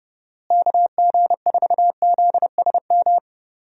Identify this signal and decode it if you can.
Morse code, heard as a keyed tone, about 30 wpm, 700 Hz. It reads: KG4ZSM
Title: USA Call Signs - 6 Characters - Repeated 2x - 30wpm